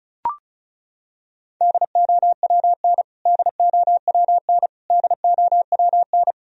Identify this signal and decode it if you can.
Morse code, heard as a keyed tone, about 36 wpm, 700 Hz. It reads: DOWN DOWN DOWN
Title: Top 100 Words Repeated 35wpm